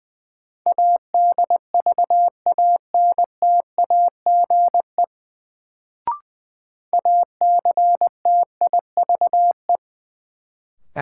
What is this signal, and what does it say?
Morse code, heard as a keyed tone, about 20 wpm, 700 Hz. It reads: ADVANTAGE ACTIVE
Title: Sentences from Top 900 Words - New Word Drill - 20wpm